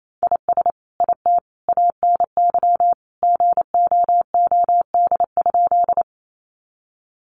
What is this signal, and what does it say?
Morse code, heard as a keyed tone, about 28 wpm, 700 Hz. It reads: IS IT ANY GOOD?
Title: Sentences from Top 200 Words 28wpm